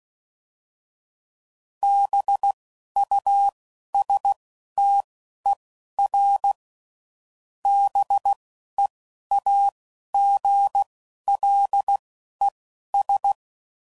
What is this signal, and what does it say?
Morse code, heard as a keyed tone, about 16 wpm, spaced as for 11 wpm, 800 Hz. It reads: BUSTER BEAGLES